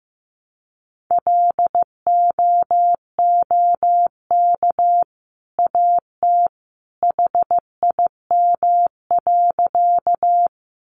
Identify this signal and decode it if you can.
Morse code, heard as a keyed tone, about 15 wpm, 700 Hz. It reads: LOOK AT HIM.
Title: Sentences from Top 100 Words 15wpm